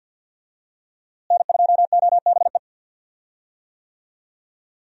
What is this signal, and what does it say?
Morse code, heard as a keyed tone, about 47 wpm, 700 Hz. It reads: N2OBE
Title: USA Call Signs - 5 Characters - Repeated 3x - 50wpm